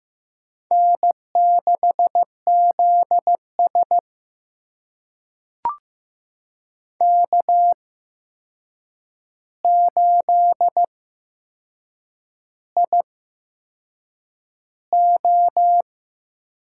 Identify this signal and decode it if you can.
Morse code, heard as a keyed tone, about 15 wpm, spaced as for 6 wpm, 700 Hz. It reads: N6ZS K8IO
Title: USA Call Signs - 4 Characters - 8x Character Spacing - 15wpm